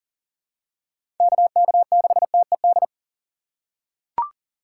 Morse code, heard as a keyed tone, about 39 wpm, 700 Hz. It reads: KK6TED
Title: USA Call Signs - 4x Character Spacing - 40wpm